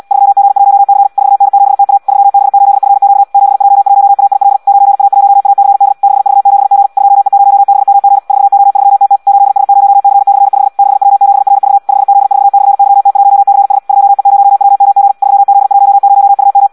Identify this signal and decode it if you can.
Morse code, heard as a keyed tone, about 74 wpm, 800 Hz. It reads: 3G1C 4N7TA 3B8FQ 7J1AIL 9A1ACD 6V7G 2E0BMG 4O3TT 8S0C/5 6W7RV 5Z4/YT1CS 2E0WMG 3Z10UM
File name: Audio practica CW / qrq-tx-74 2